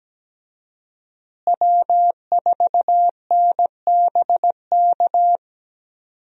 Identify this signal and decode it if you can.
Morse code, heard as a keyed tone, about 17 wpm, 700 Hz. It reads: W4NBK
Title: USA Call Signs - 5 Characters - Repeated 2x - 17wpm